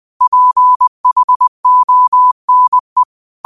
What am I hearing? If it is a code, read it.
Morse code, heard as a keyed tone, about 20 wpm, 1000 Hz. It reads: PHONE